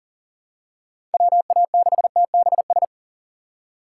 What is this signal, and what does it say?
Morse code, heard as a keyed tone, about 40 wpm, 700 Hz. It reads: WA6TBS
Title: USA Call Signs - 6 Characters - 2x Character Spacing - 40wpm